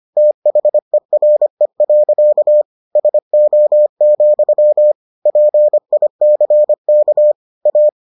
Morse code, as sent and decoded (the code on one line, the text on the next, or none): - .... .-. .-.-.- / ... --- --..-- / .--. .. -.-. -.- / .-
THERE. SO, PICK A